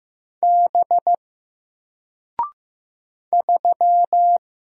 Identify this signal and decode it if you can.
Morse code, heard as a keyed tone, about 15 wpm, 700 Hz. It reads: B 3
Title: Single Letter-Number 15wpm